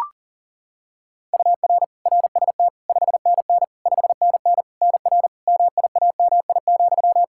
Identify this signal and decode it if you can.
Morse code, heard as a keyed tone, about 40 wpm, 700 Hz. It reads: UR RST 5NN 5NN NR MIAMI,
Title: K1USN SST QSO Practice 40wpm